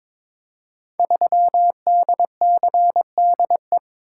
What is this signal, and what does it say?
Morse code, heard as a keyed tone, about 22 wpm, 700 Hz. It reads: 3DCDE
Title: Code Groups - 5 Letter:Numbers - 22wpm